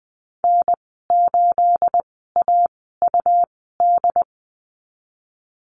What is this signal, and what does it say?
Morse code, heard as a keyed tone, about 20 wpm, spaced as for 15 wpm, 700 Hz. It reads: N8AUD
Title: 219 Lesson 35g - Number 0 - Call Signs 20wpm